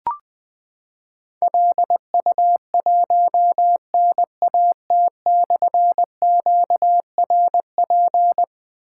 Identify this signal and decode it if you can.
Morse code, heard as a keyed tone, about 20 wpm, 700 Hz. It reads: LU1NAT/QRP
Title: Difficult Call Signs 20wpm